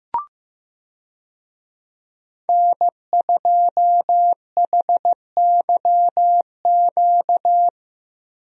Morse code, heard as a keyed tone, about 16 wpm, 700 Hz. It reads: N2HYQ
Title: USA Call Signs - 5 Characters - Repeated 2x - 15wpm